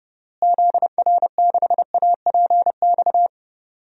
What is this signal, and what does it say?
Morse code, heard as a keyed tone, about 30 wpm, 700 Hz. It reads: ZR6APX
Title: DXCC-100 Call Signs - 6 Characters - Repeated 2x - 30wpm